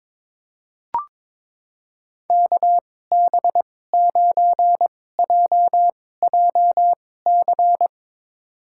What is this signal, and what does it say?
Morse code, heard as a keyed tone, about 21 wpm, spaced as for 16 wpm, 700 Hz. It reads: KB9JJC
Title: USA Call Signs - 6 Characters - 2x Character Spacing - 22wpm